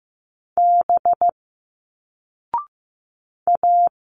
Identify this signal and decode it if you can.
Morse code, heard as a keyed tone, about 15 wpm, 700 Hz. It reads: B A
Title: Single Letters 15wpm